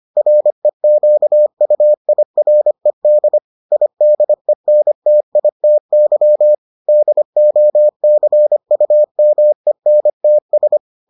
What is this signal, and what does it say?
Morse code, heard as a keyed tone, about 25 wpm, 600 Hz. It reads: REQUIRED IDENTITY DOCUMENTS